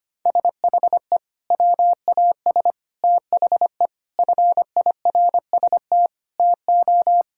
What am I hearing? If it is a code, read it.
Morse code, heard as a keyed tone, about 25 wpm, 700 Hz. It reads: SHE WAS THE FIRST TO
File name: practice / 25wpm / Sentences with 3-Word Phrases - 25wpm